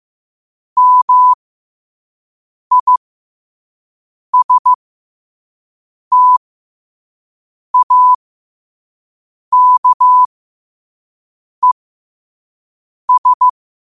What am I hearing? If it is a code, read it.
Morse code, heard as a keyed tone, about 15 wpm, spaced as for 5 wpm, 1000 Hz. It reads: MISTAKES
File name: qotdfiles / 5wpm